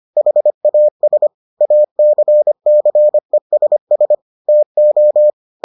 Morse code, heard as a keyed tone, about 25 wpm, 600 Hz. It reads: HAS ACCESS TO